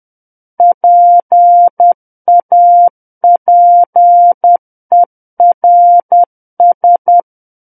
Morse code, heard as a keyed tone, about 10 wpm, 700 Hz. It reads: PAPERS
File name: Book One / 01-Chapter 1 - 10 WPM